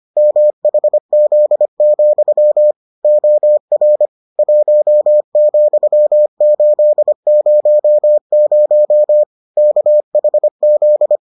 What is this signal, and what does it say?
Morse code, heard as a keyed tone, about 25 wpm, 600 Hz. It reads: MHZ, OR 1,800 KHZ